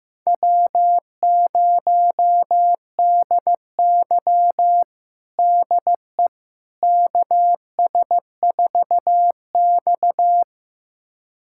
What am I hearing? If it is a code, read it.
Morse code, heard as a keyed tone, about 16 wpm, 700 Hz. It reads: W0DY DE KS4X
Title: USA Callsign DE Callsign - 4 Characters - 15wpm